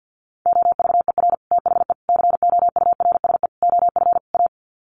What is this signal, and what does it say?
Morse code, heard as a keyed tone, about 50 wpm, 700 Hz. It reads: OVER THE COURSE OF A
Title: Sets of 5 Words Encore 50wpm